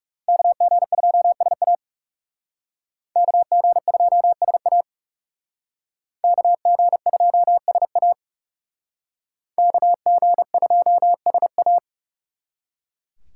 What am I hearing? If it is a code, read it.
Morse code, heard as a keyed tone, about 36 wpm, 700 Hz. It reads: KG2SA KG2SA KG2SA KG2SA
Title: Speed Racing - USA 2x2 Call Signs 30wpm